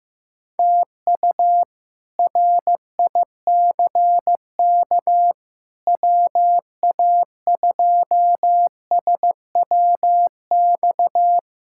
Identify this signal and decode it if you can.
Morse code, heard as a keyed tone, about 16 wpm, 700 Hz. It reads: TU RICK WA2SWX
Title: K1USN SST QSO Practice - 15wpm